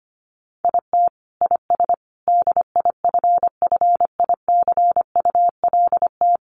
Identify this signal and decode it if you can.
Morse code, heard as a keyed tone, about 25 wpm, 700 Hz. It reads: IT IS DIFFICULT
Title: Sets of 4 Words 25wpm